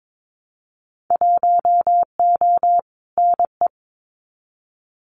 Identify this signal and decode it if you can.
Morse code, heard as a keyed tone, about 22 wpm, 700 Hz. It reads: 1O NE
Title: Winter Field Day Exchanges 22wpm